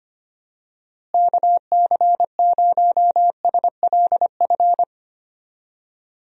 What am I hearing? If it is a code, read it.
Morse code, heard as a keyed tone, about 25 wpm, 700 Hz. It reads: KC0SLF
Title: USA Call Signs - 6 Characters - Repeated 3x - 25wpm